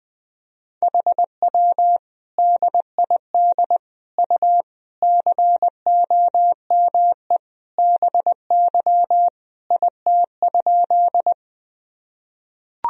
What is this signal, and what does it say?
Morse code, heard as a keyed tone, about 21 wpm, 700 Hz. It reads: HW DID U COME BY IT?
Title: Sentences from Top 100 Words and Abbreviations 20wpm